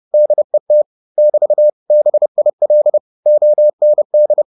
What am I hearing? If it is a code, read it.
Morse code, heard as a keyed tone, about 30 wpm, 600 Hz. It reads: DET <BT> BIL OND
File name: Korte ord DK 30wpm